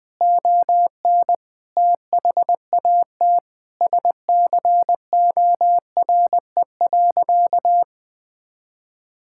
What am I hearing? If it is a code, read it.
Morse code, heard as a keyed tone, about 20 wpm, 700 Hz. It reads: ON THAT SCORE.
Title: Sentences from Top 1600 Words 20wpm